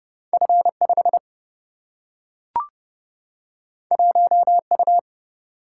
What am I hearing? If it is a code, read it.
Morse code, heard as a keyed tone, about 31 wpm, 700 Hz. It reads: F 5 1 U
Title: Single-Letter Single-Number 30wpm